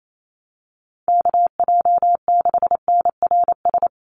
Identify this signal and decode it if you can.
Morse code, heard as a keyed tone, about 28 wpm, 700 Hz. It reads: KJ6NRS
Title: USA Call Signs 28wpm